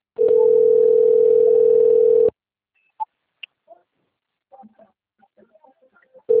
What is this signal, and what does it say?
Ringback tone (North America), 440 + 480 Hz.
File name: rg-1600-unknown- - - .91672